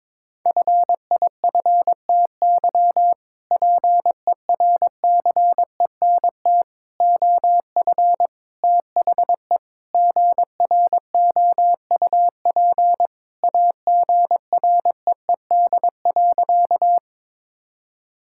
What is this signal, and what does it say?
Morse code, heard as a keyed tone, about 22 wpm, 700 Hz. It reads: FIFTY PERCENT OF THE GROUP AGREED.
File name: practice / 22wpm / Sentences with 3-Word Phrases - 22wpm